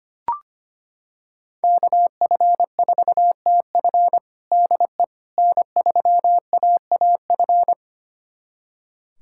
Morse code, heard as a keyed tone, about 25 wpm, 700 Hz. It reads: KF4TF DE N3AAF
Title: USA Callsign DE Callsign - 5 Characters - 25wpm